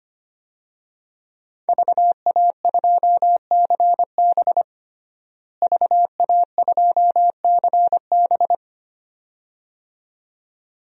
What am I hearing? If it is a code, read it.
Morse code, heard as a keyed tone, about 25 wpm, 700 Hz. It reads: VA2CB VA2CB
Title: DXCC-100 Call Signs - 5 Characters - Repeated 2x - 25wpm